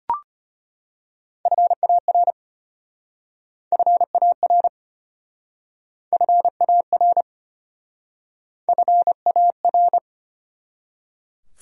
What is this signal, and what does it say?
Morse code, heard as a keyed tone, about 33 wpm, 700 Hz. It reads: FAR FAR FAR FAR